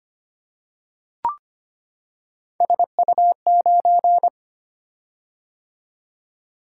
Morse code, heard as a keyed tone, about 25 wpm, 700 Hz. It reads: SU9
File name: Three Letter-Number Combinations - 25wpm